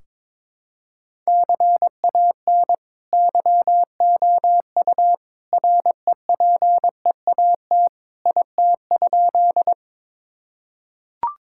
Morse code, heard as a keyed tone, about 22 wpm, 700 Hz. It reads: CAN YOU REPEAT IT?
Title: Sentences from Top 1600 Words 22wpm